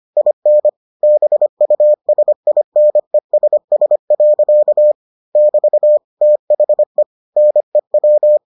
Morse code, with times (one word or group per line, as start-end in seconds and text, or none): 0.2-0.7 IN
1.0-4.9 BUSINESS.
5.4-6.0 <BT>
6.2-7.0 THE
7.4-8.4 NEW